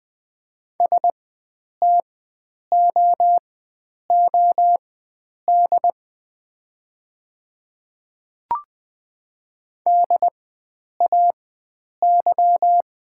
Morse code, heard as a keyed tone, about 20 wpm, spaced as for 9 wpm, 700 Hz. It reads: STOOD DAY
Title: 4x Character Spacing - Top 500 Words - 20wpm